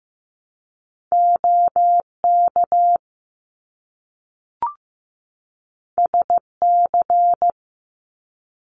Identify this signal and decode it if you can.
Morse code, heard as a keyed tone, about 15 wpm, 700 Hz. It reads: OK SC
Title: US States 15wpm